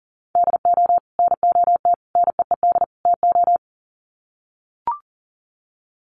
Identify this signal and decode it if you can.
Morse code, heard as a keyed tone, about 40 wpm, 700 Hz. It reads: DO NOT NEED TO
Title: Sets of 4 Words 40wpm